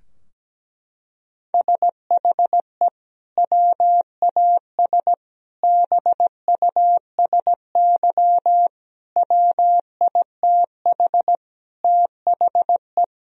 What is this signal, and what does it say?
Morse code, heard as a keyed tone, about 17 wpm, 700 Hz. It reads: SHE WAS BUSY WITH THE
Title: Sentences from Top 1000 Words - Review Part 2 - 17wpm